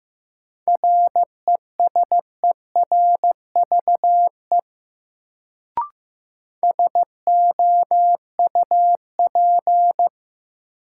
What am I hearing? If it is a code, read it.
Morse code, heard as a keyed tone, about 15 wpm, 700 Hz. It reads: RESERVE SOUP
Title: Sentences from Top 1000 Words - Review Part 6 - New Word Drill - 15wpm